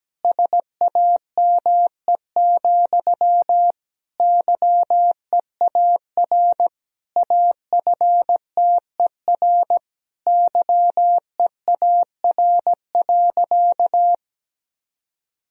Morse code, heard as a keyed tone, about 17 wpm, 700 Hz. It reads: SAME, YEAR AFTER YEAR.
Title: Sentences from Top 1000 Words - Review Part 1 - 17wpm